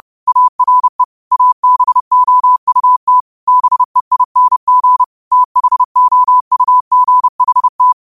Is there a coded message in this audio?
Yes: Morse code from 0.3 to 7.9 s, heard as a keyed tone, about 30 wpm, 1000 Hz. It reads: ARE ABOUT BEING THOUGHT